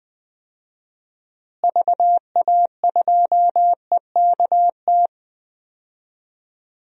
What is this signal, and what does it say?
Morse code, heard as a keyed tone, about 20 wpm, 700 Hz. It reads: VA2EKT
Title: DXCC-100 Call Signs - 6 Characters - Repeated 2x - 20wpm